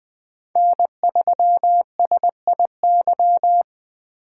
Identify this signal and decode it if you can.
Morse code, heard as a keyed tone, about 20 wpm, 700 Hz. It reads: N3SIY
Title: Speed Racing - USA 1x3 Call Signs 17wpm